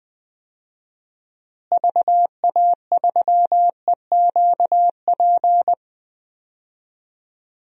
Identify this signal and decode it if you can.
Morse code, heard as a keyed tone, about 21 wpm, 700 Hz. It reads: VA3EQP
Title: DXCC-100 Call Signs - 6 Characters - Repeated 3x - 20wpm